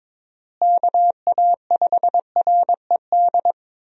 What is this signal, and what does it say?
Morse code, heard as a keyed tone, about 22 wpm, 700 Hz. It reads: KA5RED
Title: USA Call Signs - 6 Characters - Repeated 3x - 22wpm